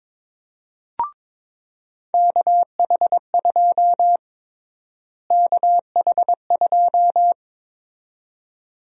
Morse code, heard as a keyed tone, about 22 wpm, 700 Hz. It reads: KH2 KH2
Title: USA Call Sign Prefixes - Repeated 2x - 22wpm